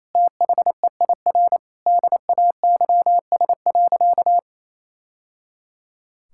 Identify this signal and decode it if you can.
Morse code, heard as a keyed tone, about 28 wpm, 700 Hz. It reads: THEIR DAYS.
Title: Sentences from Top 1000 Words - Review Part 5 - 28wpm